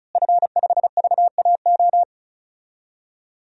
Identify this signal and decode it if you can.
Morse code, heard as a keyed tone, about 36 wpm, 700 Hz. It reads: F5VAO
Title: DXCC-100 Call Signs - 5 Characters - Repeated 3x - 35wpm